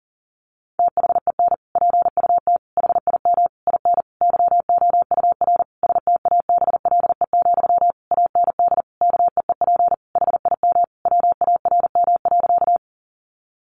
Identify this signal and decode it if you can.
Morse code, heard as a keyed tone, about 40 wpm, 700 Hz. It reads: THEN PUT HIM IN YOUR STABLE, AND KEEP HIM WARM.
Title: Sentences from Top 1500 Words 40wpm